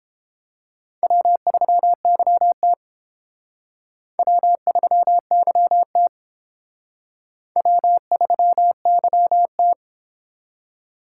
Morse code, heard as a keyed tone, about 28 wpm, 700 Hz. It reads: W3YT W3YT W3YT